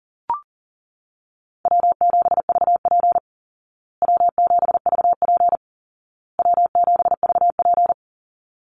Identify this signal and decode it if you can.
Morse code, heard as a keyed tone, about 40 wpm, 700 Hz. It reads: W7VP W7VP W7VP
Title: Call Signs Repeated 2x - 40wpm